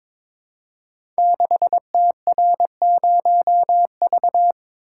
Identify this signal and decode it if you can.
Morse code, heard as a keyed tone, about 22 wpm, 700 Hz. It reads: 6TR0V
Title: Code Groups - 5 Letter:Numbers - 22wpm